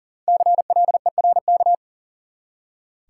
Morse code, heard as a keyed tone, about 40 wpm, 700 Hz. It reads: CLERK
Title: Five Letter Words - 40wpm